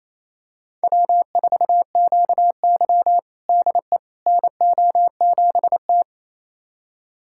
Morse code, heard as a keyed tone, about 28 wpm, 700 Hz. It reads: W4QY DE NO7T
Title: USA Callsign DE Callsign - 4 Characters - 28wpm